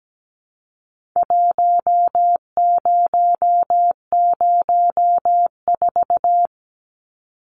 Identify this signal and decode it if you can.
Morse code, heard as a keyed tone, about 17 wpm, 700 Hz. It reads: 1004
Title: Numbers - Contest Style - 4-Digit 17wpm